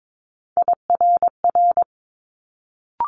Morse code, heard as a keyed tone, about 22 wpm, 700 Hz. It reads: IRR
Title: Multi-letter Phonemes Quick Fire 22wpm